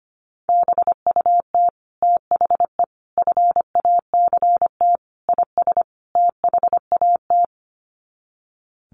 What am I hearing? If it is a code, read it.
Morse code, heard as a keyed tone, about 25 wpm, 700 Hz. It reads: BUT THE FACT IS THAT
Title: Sets of 5 Words Encore2 25wpm